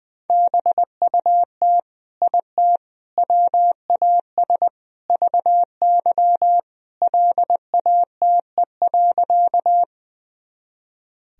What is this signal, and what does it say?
Morse code, heard as a keyed tone, about 20 wpm, 700 Hz. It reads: BUT IT WAS VY LATE.
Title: Sentences from Top 300 Words and Abbreviations 20wpm